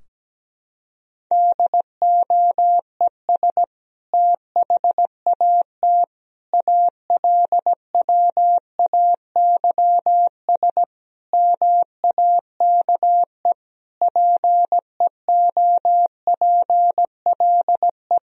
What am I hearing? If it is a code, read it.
Morse code, heard as a keyed tone, about 17 wpm, 700 Hz. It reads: DOES THAT ALWAYS MAKE PEOPLE
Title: Sentences from Top 1000 Words - Review Part 2 - 17wpm